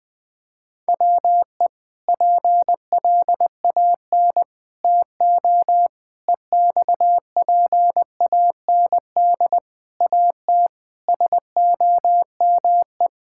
Morse code, heard as a keyed tone, about 20 wpm, 700 Hz. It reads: WE PLAN TO EXPAND AT SOME
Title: Sentences with 6-Word Phrases - Spoken Beginning - 20wpm